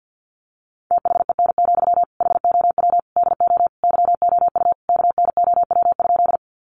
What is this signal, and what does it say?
Morse code, heard as a keyed tone, about 48 wpm, 700 Hz. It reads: THEN, HOW DO YOU KNOW?
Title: Sentences from Top 100 Words 50wpm